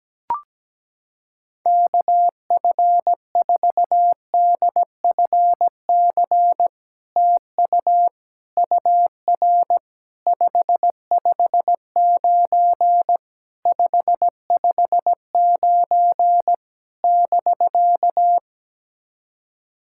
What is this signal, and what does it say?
Morse code, heard as a keyed tone, about 17 wpm, 700 Hz. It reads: KF4DFC TU UR 559 559 <BK>
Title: SOTA QSO Practice 17wpm